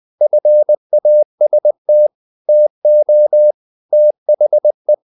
Morse code, heard as a keyed tone, about 20 wpm, 600 Hz. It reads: FAST TO THE
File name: the old man and the sea / the old man and the sea part 1 20wpm